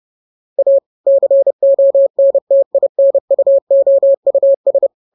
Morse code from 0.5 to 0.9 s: A